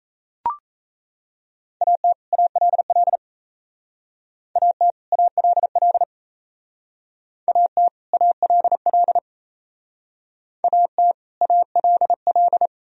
Morse code, heard as a keyed tone, about 36 wpm, 700 Hz. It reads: AT ALL AT ALL AT ALL AT ALL